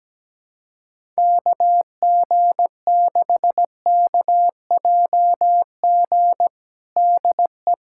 Morse code, heard as a keyed tone, about 17 wpm, 700 Hz. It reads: KG6KJG DE
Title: USA Callsign DE Callsign - 6 Characters - 17wpm